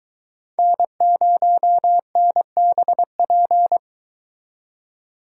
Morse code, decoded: N0NBP

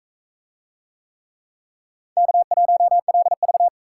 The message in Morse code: K1LV

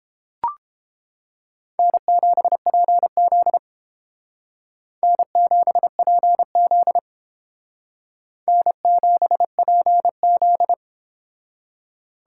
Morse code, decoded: N7PZ N7PZ N7PZ

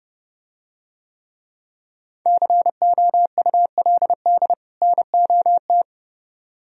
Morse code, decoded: COULD NOT

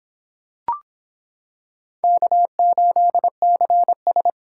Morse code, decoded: K8CS

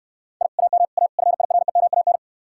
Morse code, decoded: I AM A FIREMAN

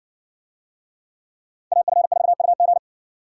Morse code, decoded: AF4UG

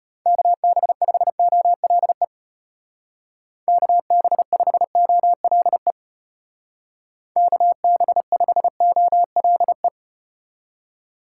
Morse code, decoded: KB5OLE KB5OLE KB5OLE

